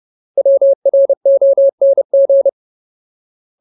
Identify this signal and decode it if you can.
Morse code, heard as a keyed tone, about 30 wpm, 550 Hz. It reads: WRONG
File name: Ultimate Word List / 30WPM Ultimate